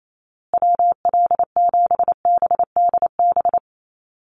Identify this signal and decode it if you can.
Morse code, heard as a keyed tone, about 28 wpm, 700 Hz. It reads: WL7BDB
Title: USA Call Signs Repeated 2x 28wpm